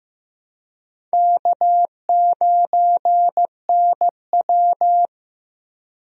Morse code, heard as a keyed tone, about 15 wpm, 700 Hz. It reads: K9NW
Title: USA Call Signs - 4 Characters - Repeated 2x - 15wpm